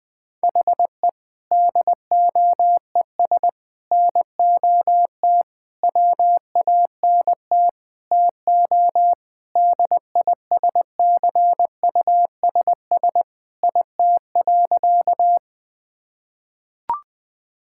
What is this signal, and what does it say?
Morse code, heard as a keyed tone, about 21 wpm, 700 Hz. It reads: HE DOES NOT WANT TO DISCUSS IT.
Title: Sentences with 5-Word Phrases - 20wpm